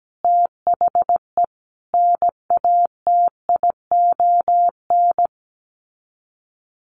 Morse code, heard as a keyed tone, about 17 wpm, 700 Hz. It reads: THE NATION
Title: Sets of 4 Words 17wpm